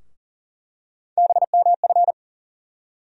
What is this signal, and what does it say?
Morse code, heard as a keyed tone, about 40 wpm, 700 Hz. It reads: BMF
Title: USA Call Sign Postfixes - 3 Letters 40wpm